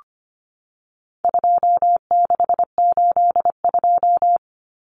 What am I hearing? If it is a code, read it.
Morse code, heard as a keyed tone, about 25 wpm, 700 Hz. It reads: 2682
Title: Common Cut Numbers - 4-Digit 25wpm